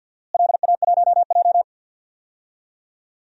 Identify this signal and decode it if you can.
Morse code, heard as a keyed tone, about 52 wpm, 700 Hz. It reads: LA1J
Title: DXCC-100 Call Signs - 4 Characters - Repeated 3x - 50wpm